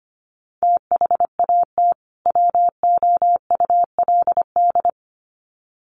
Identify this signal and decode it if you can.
Morse code, heard as a keyed tone, about 25 wpm, 700 Hz. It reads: THAT WOULD
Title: Sets of 2 Words 25wpm